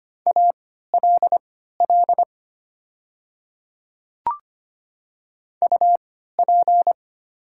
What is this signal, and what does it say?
Morse code, heard as a keyed tone, about 25 wpm, spaced as for 14 wpm, 700 Hz. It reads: ALL UP
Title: 3x Character Spacing - Top 100 Words - 25wpm